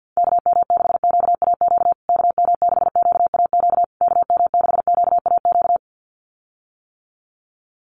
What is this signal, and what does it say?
Morse code, heard as a keyed tone, about 50 wpm, 700 Hz. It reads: KM6QAQ KM6QAQ KM6QAQ
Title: USA Call Signs Repeated 3x 50wpm